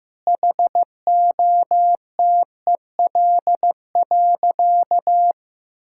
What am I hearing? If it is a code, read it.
Morse code, heard as a keyed tone, about 15 wpm, 700 Hz. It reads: HOTEL.
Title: Sentences from Top 1600 Words 15wpm